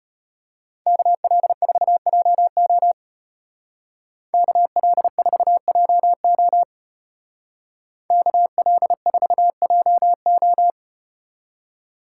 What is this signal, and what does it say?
Morse code, heard as a keyed tone, about 32 wpm, 700 Hz. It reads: KL4JO KL4JO KL4JO